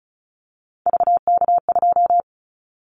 Morse code, heard as a keyed tone, about 36 wpm, 700 Hz. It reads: VK2
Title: Callsign Prefixes for Top 100 DXCC - 35wpm